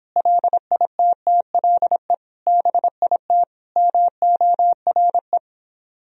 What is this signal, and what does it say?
Morse code, heard as a keyed tone, about 26 wpm, 700 Hz. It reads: LITTLE BIT MORE